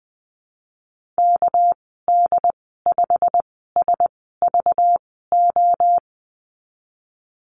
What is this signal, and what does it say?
Morse code, heard as a keyed tone, about 20 wpm, spaced as for 15 wpm, 700 Hz. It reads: KD5SVO
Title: 200 Lesson 32g - Letter k - Call Signs 20wpm